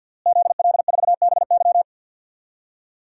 Morse code, heard as a keyed tone, about 51 wpm, 700 Hz. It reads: ZL4BY
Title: DXCC-100 Call Signs - 5 Characters - Repeated 2x - 50wpm